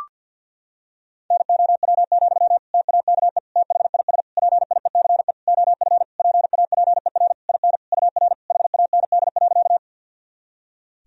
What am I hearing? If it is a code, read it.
Morse code, heard as a keyed tone, about 51 wpm, 700 Hz. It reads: NOW, TAKE THIS PIECE OF PAPER IN UR HAND.
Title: Sentences from Top 400 Words and Abbreviations 50wpm